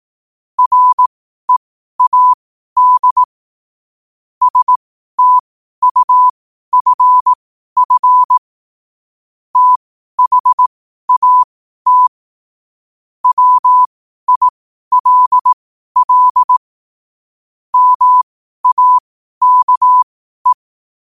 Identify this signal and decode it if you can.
Morse code, heard as a keyed tone, about 18 wpm, spaced as for 12 wpm, 1000 Hz. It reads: READ STUFF THAT WILL MAKE
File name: qotdfiles / 12wpm